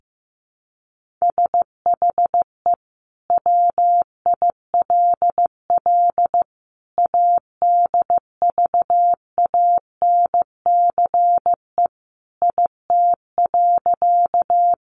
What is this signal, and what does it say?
Morse code, heard as a keyed tone, about 15 wpm, 700 Hz. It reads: SHE WILL ADVANCE IT.
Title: Sentences from Top 1400 Words 15wpm